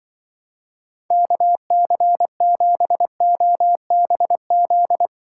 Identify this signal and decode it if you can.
Morse code, heard as a keyed tone, about 24 wpm, 700 Hz. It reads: KC7OBZ
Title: Speed Racing - USA 2x3 Call Signs 20wpm